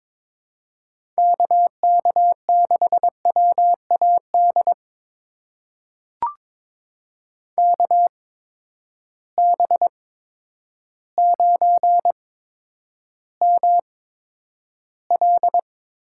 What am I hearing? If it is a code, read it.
Morse code, heard as a keyed tone, about 21 wpm, spaced as for 10 wpm, 700 Hz. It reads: KK6WAD KB9ML
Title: USA Call Signs - 8x Character Spacing - 22wpm